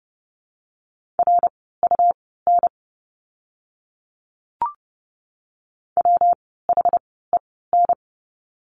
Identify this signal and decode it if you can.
Morse code, heard as a keyed tone, about 30 wpm, spaced as for 17 wpm, 700 Hz. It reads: RUN WHEN
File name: course / 30:10wpm / 115 Lesson 19f - Letter w - Words 30wpm